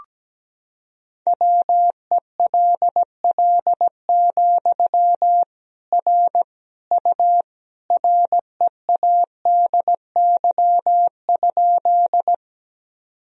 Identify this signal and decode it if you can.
Morse code, heard as a keyed tone, about 17 wpm, 700 Hz. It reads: WELL, R U READY?
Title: Sentences from Top 400 Words and Abbreviations 17wpm